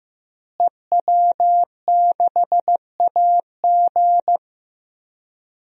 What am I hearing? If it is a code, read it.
Morse code, heard as a keyed tone, about 15 wpm, 700 Hz. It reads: EW6AG
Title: DXCC-100 Call Signs - 5 Characters - Repeated 3x - 15wpm